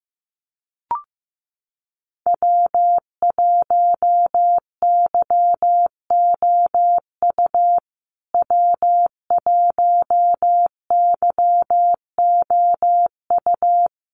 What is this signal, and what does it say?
Morse code, heard as a keyed tone, about 15 wpm, 700 Hz. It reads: W1YOU W1YOU
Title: USA Call Signs Repeated 3x 15wpm